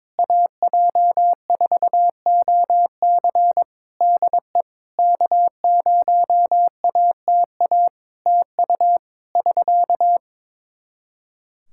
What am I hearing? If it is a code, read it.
Morse code, heard as a keyed tone, about 22 wpm, 700 Hz. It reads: AJ4OC DE K0ATA TU <SK>
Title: Practice QSO - Exchange 4 - Ending - 22wpm